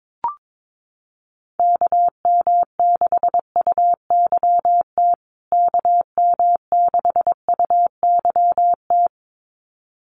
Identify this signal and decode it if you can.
Morse code, heard as a keyed tone, about 22 wpm, 700 Hz. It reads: KM6UYT KM6UYT
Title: USA Call Signs Repeated 2x 22wpm